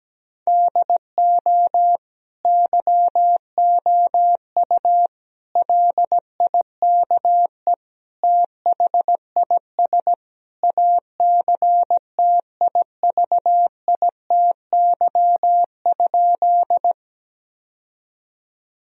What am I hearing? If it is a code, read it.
Morse code, heard as a keyed tone, about 17 wpm, 700 Hz. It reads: DO YOU LIKE THIS ACTIVITY?
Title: Sentences from Top 1000 Words - Review Part 4 - 17wpm